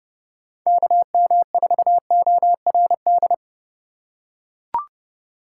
Morse code, heard as a keyed tone, about 32 wpm, 700 Hz. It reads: KM4ORD
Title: USA Call Signs - 6 Characters - 4x Character Spacing - 30wpm